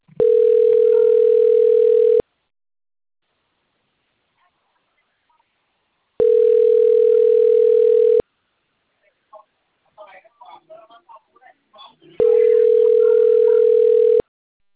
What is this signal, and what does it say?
Ringback tone (North America), 440 + 480 Hz.